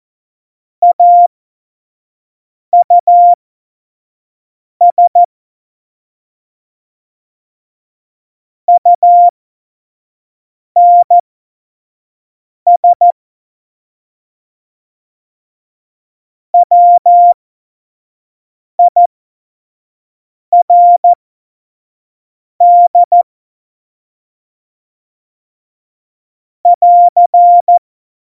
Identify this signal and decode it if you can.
Morse code, heard as a keyed tone, about 14 wpm, spaced as for 5 wpm, 700 Hz. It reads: AUS UNS WIRD <AR>
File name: Morsekurs / Witze / Witz 27. November 2018, 25 F70